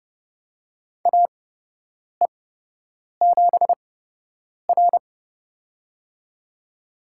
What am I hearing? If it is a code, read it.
Morse code, heard as a keyed tone, about 31 wpm, spaced as for 8 wpm, 700 Hz. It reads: AE7R
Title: USA Call Signs - 4 Characters - 8x Character Spacing - 30wpm